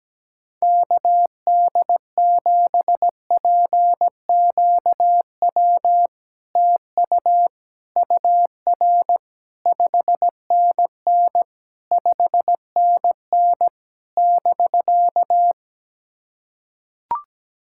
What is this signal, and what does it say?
Morse code, heard as a keyed tone, about 17 wpm, 700 Hz. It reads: KD7PQW TU UR 5NN 5NN <BK>
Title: SOTA QSO Practice 17wpm